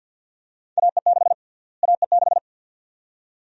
Morse code, heard as a keyed tone, about 51 wpm, 700 Hz. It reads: AE6 AE6
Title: USA Call Sign Prefixes - Repeated 2x - 50wpm